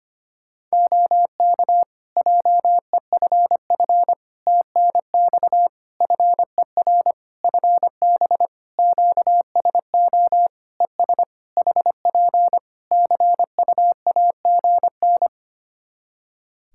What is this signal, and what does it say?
Morse code, heard as a keyed tone, about 25 wpm, 700 Hz. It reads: OK JEFF TNX FER FB QSO ES HP CUAGN
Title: K1USN SST QSO Practice 25wpm